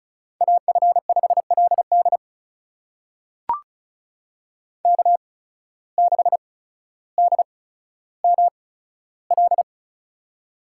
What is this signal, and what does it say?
Morse code, heard as a keyed tone, about 36 wpm, spaced as for 15 wpm, 700 Hz. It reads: AF5LD K6DML
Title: USA Call Signs - 5 Characters - 8x Character Spacing - 35wpm